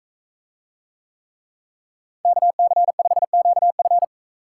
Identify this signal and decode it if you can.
Morse code, heard as a keyed tone, about 42 wpm, 700 Hz. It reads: KC5QF